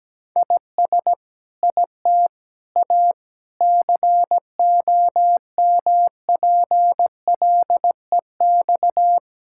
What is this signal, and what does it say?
Morse code, heard as a keyed tone, about 17 wpm, 700 Hz. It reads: IS IT A COMPLEX
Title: Sentences from Top 1000 Words - Review Part 6 - 17wpm